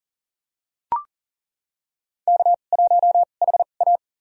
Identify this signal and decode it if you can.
Morse code, heard as a keyed tone, about 40 wpm, spaced as for 29 wpm, 700 Hz. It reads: K1HA
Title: USA Call Signs - 4 Characters - 2x Character Spacing - 40wpm